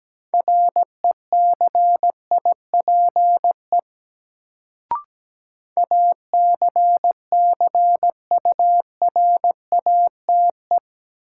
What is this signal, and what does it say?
Morse code, heard as a keyed tone, about 17 wpm, 700 Hz. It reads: RECIPE ACCURATE
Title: Sentences from Top 1200 Words - New Word Drill Alt - 17wpm